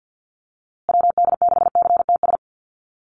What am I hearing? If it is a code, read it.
Morse code, heard as a keyed tone, about 50 wpm, 700 Hz. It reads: WD6CTS
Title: USA Call Signs Repeated 3x 50wpm